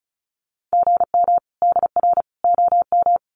Morse code, heard as a keyed tone, about 34 wpm, 700 Hz. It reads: GM DR OM
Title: Common QSO Phrases 35wpm